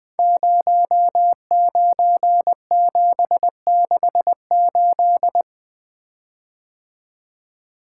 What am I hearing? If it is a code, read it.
Morse code, heard as a keyed tone, about 20 wpm, 700 Hz. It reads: 09768
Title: Code Groups - 5 Letter:Numbers - 20wpm